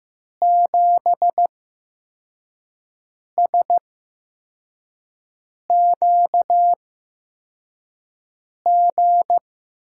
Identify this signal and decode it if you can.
Morse code, heard as a keyed tone, about 15 wpm, spaced as for 4 wpm, 700 Hz. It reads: 7SQG
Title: USA Call Signs - 6 Characters - 8x Character Spacing - 15wpm